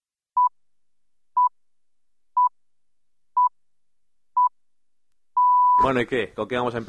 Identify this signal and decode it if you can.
Radio time signal (time pips), with speech. It is heard from 0.4 to 5.9 s.